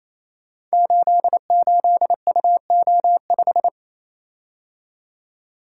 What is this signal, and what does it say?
Morse code, heard as a keyed tone, about 28 wpm, 700 Hz. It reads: 88UO5
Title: Code Groups - 5 Letter:Numbers - 28wpm